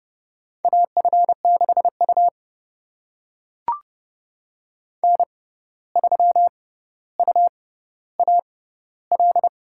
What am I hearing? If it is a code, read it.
Morse code, heard as a keyed tone, about 30 wpm, spaced as for 14 wpm, 700 Hz. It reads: AF6U N3UAL
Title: USA Call Signs - 6x Character Spacing - 30wpm